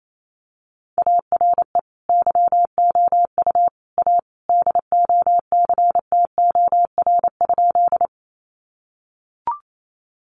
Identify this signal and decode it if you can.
Morse code, heard as a keyed tone, about 28 wpm, 700 Hz. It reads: ARE YOU A DOCTOR?
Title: Sentences from Top 1500 Words 28wpm